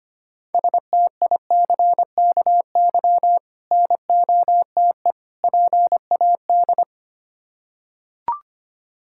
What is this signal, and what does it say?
Morse code, heard as a keyed tone, about 25 wpm, 700 Hz. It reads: STICKY NOTE PAD
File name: practice / 25wpm / Difficult Words with Hints - 25wpm